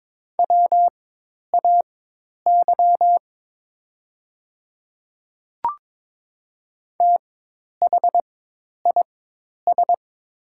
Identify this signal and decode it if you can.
Morse code, heard as a keyed tone, about 22 wpm, spaced as for 10 wpm, 700 Hz. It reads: WAY THIS
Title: 4x Character Spacing - Top 100 Words - 22wpm